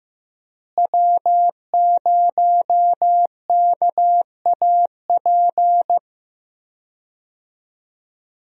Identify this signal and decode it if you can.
Morse code, heard as a keyed tone, about 15 wpm, 700 Hz. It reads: W0KAP
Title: USA Call Signs - 5 Characters - Repeated 3x - 15wpm